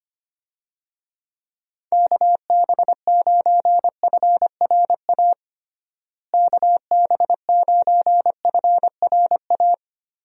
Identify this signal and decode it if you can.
Morse code, heard as a keyed tone, about 25 wpm, 700 Hz. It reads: KB9FRA KB9FRA
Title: USA Call Signs - 6 Characters - Repeated 3x - 25wpm